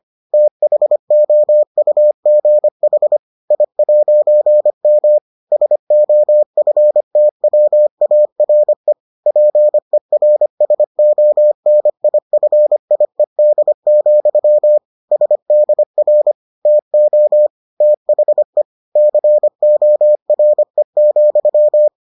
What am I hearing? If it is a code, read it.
Morse code, heard as a keyed tone, about 25 wpm, 600 Hz. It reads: THOUGH I'M SOFTWARE PERSONIFIED, SDR TO THE CORE,